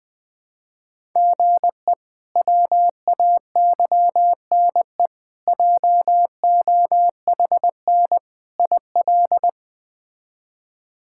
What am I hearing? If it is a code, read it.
Morse code, heard as a keyed tone, about 20 wpm, 700 Hz. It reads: GE WAYNE JOHN IL
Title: K1USN SST QSO Practice - 20wpm